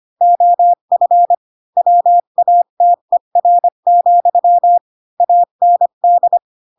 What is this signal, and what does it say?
Morse code, heard as a keyed tone, about 25 wpm, 700 Hz. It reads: OF WATER, AND
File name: Pinocchio / Pinocchio ch8 20 25 ik1hge